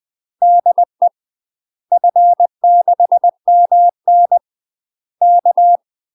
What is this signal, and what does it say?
Morse code, heard as a keyed tone, about 20 wpm, 700 Hz. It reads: DE F6MN K
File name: UK MASTER QSOs TRAINING / 20WPM / UK MASTER QSOs 0015